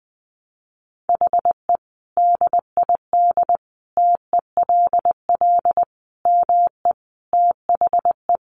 Morse code, heard as a keyed tone, about 19 wpm, 700 Hz. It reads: HE DID TELL ME THE
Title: Sentences from Top 1000 Words 20wpm